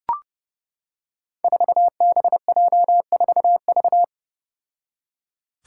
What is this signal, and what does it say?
Morse code, heard as a keyed tone, about 30 wpm, 700 Hz. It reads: 4BJ4V
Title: Five Challenging Letter-Number Combinations 30wpm